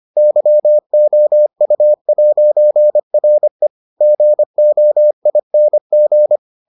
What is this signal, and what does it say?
Morse code, heard as a keyed tone, about 25 wpm, 600 Hz. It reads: YOU'RE GOING